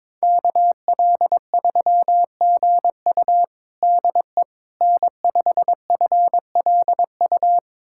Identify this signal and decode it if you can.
Morse code, heard as a keyed tone, about 22 wpm, 700 Hz. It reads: KL3GU DE N5FLU
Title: USA Callsign DE Callsign - 5 Characters - 22wpm